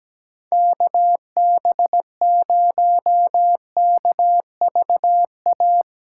Morse code, heard as a keyed tone, about 17 wpm, 700 Hz. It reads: KB0KVA
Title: USA Callsign DE Callsign - 6 Characters - 17wpm